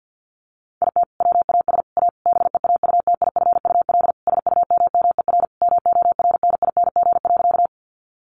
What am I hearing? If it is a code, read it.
Morse code, heard as a keyed tone, about 48 wpm, 700 Hz. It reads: IT WAS A BEAUTIFUL SUMMER MORNING.
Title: Sentences from Top 1100 Words 50wpm